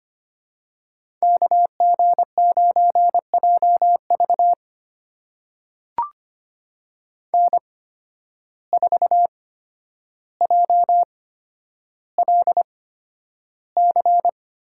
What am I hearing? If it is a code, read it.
Morse code, heard as a keyed tone, about 25 wpm, spaced as for 11 wpm, 700 Hz. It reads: KG9JV N4JLC
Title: USA Call Signs - 5 Characters - 8x Character Spacing - 25wpm